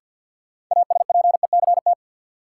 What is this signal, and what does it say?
Morse code, heard as a keyed tone, about 53 wpm, 700 Hz. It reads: ASPECT
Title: Sentences from Top 1300 Words - New Word Drill - 50wpm